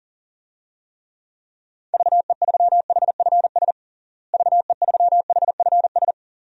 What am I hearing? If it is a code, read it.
Morse code, heard as a keyed tone, about 37 wpm, 700 Hz. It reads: VE3HFS VE3HFS
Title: DXCC-100 Call Signs - 6 Characters - Repeated 3x - 40wpm